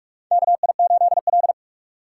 Morse code, heard as a keyed tone, about 45 wpm, 700 Hz. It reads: KI8L
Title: USA Call Signs - 4 Characters - 6x Character Spacing - 45wpm